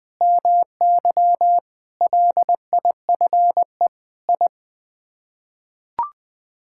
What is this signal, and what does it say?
Morse code, heard as a keyed tone, about 20 wpm, 700 Hz. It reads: MY LIFE I
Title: Sets of 8 Words - 20wpm